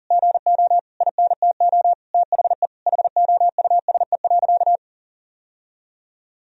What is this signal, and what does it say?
Morse code, heard as a keyed tone, about 40 wpm, 700 Hz. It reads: GO INTO THE HOUSE.
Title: Sentences from Top 1000 Words - Review Part 4 - 40wpm